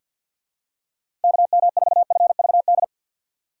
Morse code, heard as a keyed tone, about 48 wpm, 700 Hz. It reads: KM4FVD
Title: USA Call Signs - 6 Characters - 4x Character Spacing - 50wpm